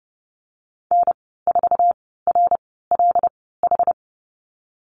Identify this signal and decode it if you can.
Morse code, heard as a keyed tone, about 30 wpm, spaced as for 17 wpm, 700 Hz. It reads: N4RLH
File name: Lesson 19g - Letter w - Call Signs 30wpm